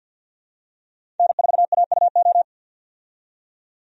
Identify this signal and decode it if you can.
Morse code, heard as a keyed tone, about 47 wpm, 700 Hz. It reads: N4AUO
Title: USA Call Signs - 5 Characters - 50wpm